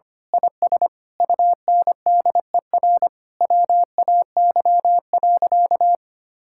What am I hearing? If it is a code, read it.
Morse code, heard as a keyed tone, about 25 wpm, 700 Hz. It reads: IS UNDER WAY.
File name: practice / 25wpm / Sentences with 9-Word Phrases - 25wpm